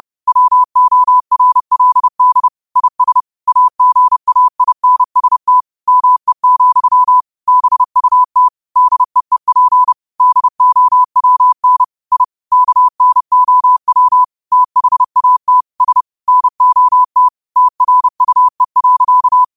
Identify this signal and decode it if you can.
Morse code, heard as a keyed tone, about 30 wpm, 1000 Hz. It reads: WORLD IS AGAINST ME, BUT DEEP DOWN I KNOW THAT S NOT TRUE.